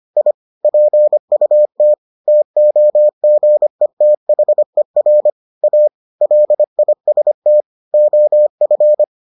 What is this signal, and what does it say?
Morse code, heard as a keyed tone, about 25 wpm, 600 Hz. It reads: I PUT TOGETHER A LIST OF